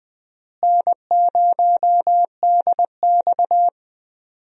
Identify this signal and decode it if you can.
Morse code, heard as a keyed tone, about 20 wpm, 700 Hz. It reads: N0DX
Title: USA Call Signs - 4 Characters - Repeated 2x - 20wpm